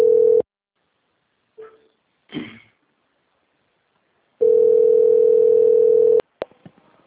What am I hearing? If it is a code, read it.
Ringback tone (North America), 440 + 480 Hz.
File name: out- - - - - .28183